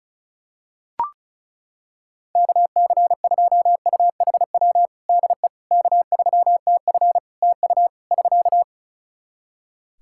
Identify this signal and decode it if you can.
Morse code, heard as a keyed tone, about 34 wpm, 700 Hz. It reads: KC2UHW DE K3TF TU <SK>
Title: Practice QSO - Exchange 4 - Ending - 35wpm